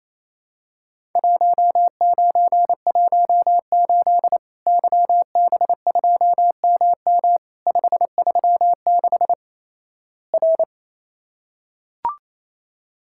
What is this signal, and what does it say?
Morse code, heard as a keyed tone, about 27 wpm, 700 Hz. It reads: 1918 YB2MM 536 R
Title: Worked All Europe DX Contest - QTC Exchange - 28wpm